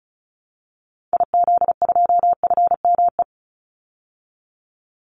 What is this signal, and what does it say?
Morse code, heard as a keyed tone, about 34 wpm, 700 Hz. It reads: IZ2FME
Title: Call Signs Repeated 3x - 35wpm